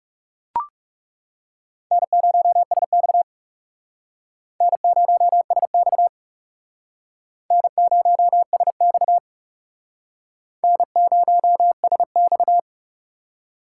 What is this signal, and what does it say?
Morse code, heard as a keyed tone, about 36 wpm, 700 Hz. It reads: N0SX N0SX N0SX N0SX
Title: Speed Racing - USA 1x2 Call Signs 30wpm